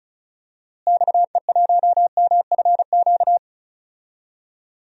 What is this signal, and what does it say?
Morse code, heard as a keyed tone, about 34 wpm, 700 Hz. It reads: XE1MFQ
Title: DXCC-100 Call Signs - 6 Characters - Repeated 2x - 35wpm